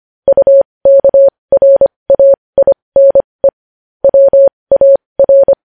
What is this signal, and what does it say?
Morse code, heard as a keyed tone, about 25 wpm, spaced as for 20 wpm, 550 Hz. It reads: UKRAINE WAR
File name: podcast / News headlines-1x-25x20